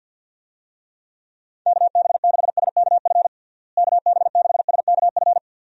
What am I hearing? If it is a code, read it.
Morse code, heard as a keyed tone, about 52 wpm, 700 Hz. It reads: KB6SKF KB6SKF
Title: USA Call Signs - 6 Characters - Repeated 2x - 50wpm